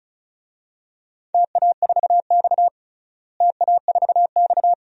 Morse code, heard as a keyed tone, about 34 wpm, 700 Hz. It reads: TA4X TA4X
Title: DXCC-100 Call Signs - 4 Characters - Repeated 2x - 35wpm